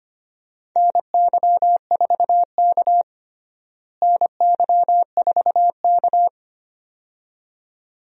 Morse code, heard as a keyed tone, about 25 wpm, 700 Hz. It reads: NY4K NY4K
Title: USA Call Signs - 4 Characters - Repeated 2x - 25wpm